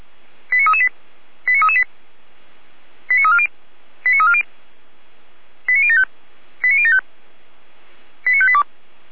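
Five-tone selective call (ZVEI), heard as quick runs of five tones, tones of about 70 ms. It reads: 89338 89338 89340 89340 89075 89075 89582
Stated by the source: ZVEI five-tone selective call 89320, received by SDR.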